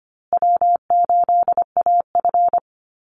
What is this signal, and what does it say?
Morse code, heard as a keyed tone, about 25 wpm, 700 Hz. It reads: W8AF
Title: USA Call Signs Repeated 2x 25wpm